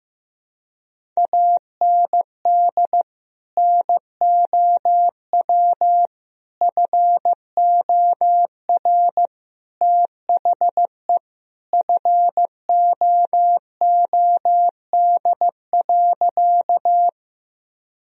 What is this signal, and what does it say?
Morse code, heard as a keyed tone, about 15 wpm, 700 Hz. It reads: AND NOW FOR THE FOOD.
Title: Sentences from Top 1000 Words - Review Part 1 - 15wpm